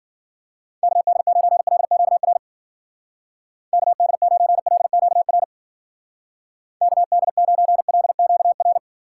Morse code, heard as a keyed tone, about 54 wpm, 700 Hz. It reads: KD9LQR KD9LQR KD9LQR